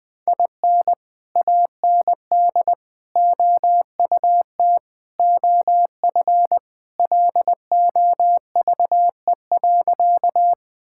Morse code, heard as a keyed tone, about 19 wpm, 700 Hz. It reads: IN AND OUT OF LOVE.
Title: Sentences with 4-Word Phrases - 20wpm